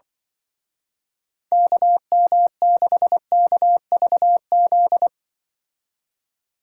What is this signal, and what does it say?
Morse code, heard as a keyed tone, about 24 wpm, 700 Hz. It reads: KM6KVZ
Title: Speed Racing - USA 2x3 Call Signs 20wpm